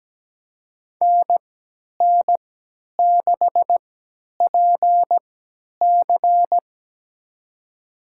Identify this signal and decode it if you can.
Morse code, heard as a keyed tone, about 17 wpm, spaced as for 10 wpm, 700 Hz. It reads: NN6PC
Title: USA Call Signs - 5 Characters - 3x Character Spacing - 17wpm